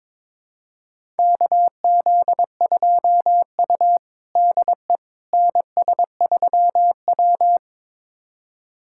Morse code, heard as a keyed tone, about 22 wpm, 700 Hz. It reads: KZ2U DE NS3W
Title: USA Callsign DE Callsign - 4 Characters - 22wpm